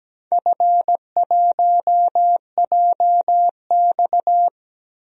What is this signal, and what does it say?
Morse code, heard as a keyed tone, about 17 wpm, 700 Hz. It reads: F1JX
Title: DXCC-100 Call Signs - 4 Characters - 17wpm